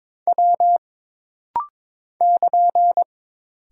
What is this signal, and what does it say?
Morse code, heard as a keyed tone, about 21 wpm, 700 Hz. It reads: W <KN>
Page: This rapid-fire timing is geared to push your ICR (Instant Character Recognition) skills.